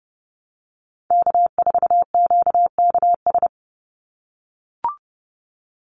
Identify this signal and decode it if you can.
Morse code, heard as a keyed tone, about 30 wpm, 700 Hz. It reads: K4QKS
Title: USA Call Signs 30wpm